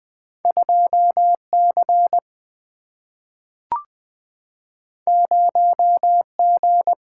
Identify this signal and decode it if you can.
Morse code, heard as a keyed tone, about 20 wpm, 700 Hz. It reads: 2 C 0 G
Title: Single-Letter Single-Number 20wpm